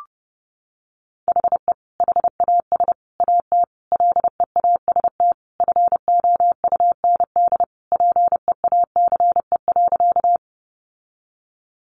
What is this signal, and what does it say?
Morse code, heard as a keyed tone, about 30 wpm, 700 Hz. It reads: HE HAS AT LEAST FOUND PEACE.
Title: Sentences from Top 1500 Words 30wpm